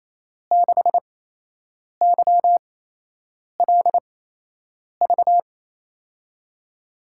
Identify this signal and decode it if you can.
Morse code, heard as a keyed tone, about 28 wpm, spaced as for 8 wpm, 700 Hz. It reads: 6YLV
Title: USA Call Signs - 5 Characters - 8x Character Spacing - 28wpm